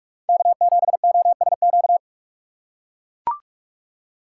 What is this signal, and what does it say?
Morse code, heard as a keyed tone, about 43 wpm, 700 Hz. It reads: K7OSQ
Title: USA Call Signs - 5 Characters - 4x Character Spacing - 45wpm